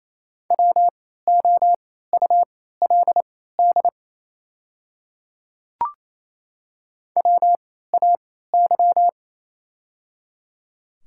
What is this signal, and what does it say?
Morse code, heard as a keyed tone, about 28 wpm, spaced as for 16 wpm, 700 Hz. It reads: WOULD WAY
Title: 3x Character Spacing - Top 100 Words - 28wpm